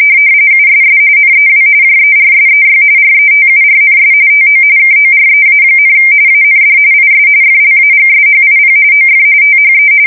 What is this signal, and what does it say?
RTTY, about 45.45 baud. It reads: USS NEW JERSEY RETURNS TO KOREA, AND IS FLAGSHIP OF THE
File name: fsk170 / battleshipnj